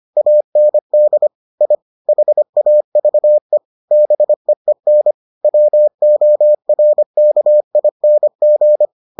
Morse code, heard as a keyed tone, about 25 wpm, 600 Hz. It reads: AND I HAVE BEEN WORKING